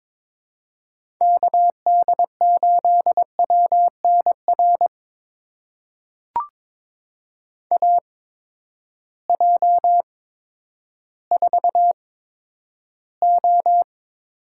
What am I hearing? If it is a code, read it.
Morse code, heard as a keyed tone, about 22 wpm, spaced as for 11 wpm, 700 Hz. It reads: KD8WNR AJ4O
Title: USA Call Signs - 8x Character Spacing - 22wpm